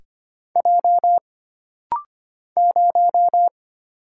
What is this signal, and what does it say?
Morse code, heard as a keyed tone, about 25 wpm, 700 Hz. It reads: J 0
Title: Single Letter-Number - Rapid-Fire 25wpm